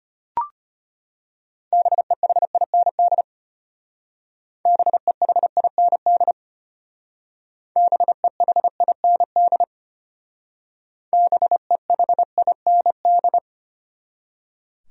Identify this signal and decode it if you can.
Morse code, heard as a keyed tone, about 32 wpm, 700 Hz. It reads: BEHIND BEHIND BEHIND BEHIND